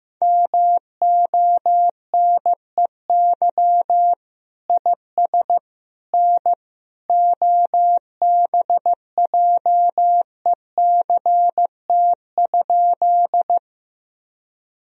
Morse code, heard as a keyed tone, about 16 wpm, 700 Hz. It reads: MONEY IS N OBJECT?
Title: Sentences from Top 500 Words and Abbreviations 15wpm